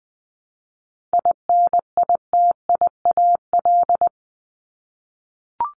Morse code, heard as a keyed tone, about 20 wpm, 700 Hz. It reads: INITIAL
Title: Top 1501-1750 Words 20wpm